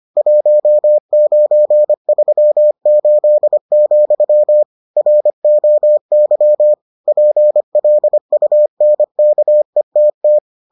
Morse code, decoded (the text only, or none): 1938, ROY PLUNKETT